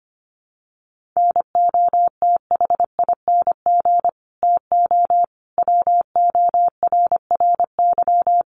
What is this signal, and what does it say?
Morse code, heard as a keyed tone, about 25 wpm, 700 Hz. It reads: NOTHING TO WORRY
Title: Sentences from Top 1100 Words 25wpm